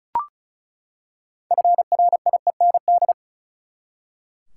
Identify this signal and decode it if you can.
Morse code, heard as a keyed tone, about 34 wpm, 700 Hz. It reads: FRIEND
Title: Six Letter Words - 35wpm